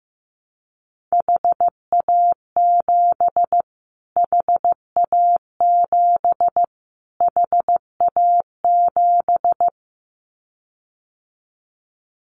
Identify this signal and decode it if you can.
Morse code, heard as a keyed tone, about 15 wpm, 700 Hz. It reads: HA7 HA7 HA7
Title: Callsign Prefixes for Top 25 DXCC - Repeated 3x - 15wpm